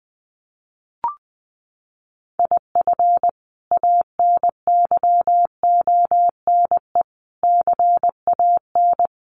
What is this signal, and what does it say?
Morse code, heard as a keyed tone, about 20 wpm, 700 Hz. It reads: IF ANYONE CAN
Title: Sentences from Top 1500 Words 20wpm